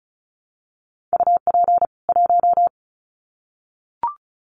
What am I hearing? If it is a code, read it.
Morse code, heard as a keyed tone, about 34 wpm, 700 Hz. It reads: UP 1
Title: Common QSO Phrases 35wpm